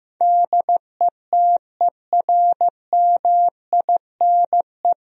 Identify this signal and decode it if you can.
Morse code, heard as a keyed tone, about 15 wpm, 700 Hz. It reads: DETERMINE
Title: Sentences from Top 1000 Words - Review Part 4 - 15wpm